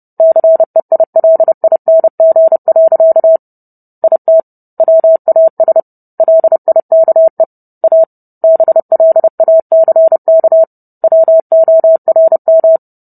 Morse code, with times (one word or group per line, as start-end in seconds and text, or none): 0.2-3.4 CEILING.
4.0-4.4 IT
4.8-5.8 WAS
6.2-7.4 LIKE
7.8-8.0 A
8.4-10.6 BLACK
11.0-12.8 WORM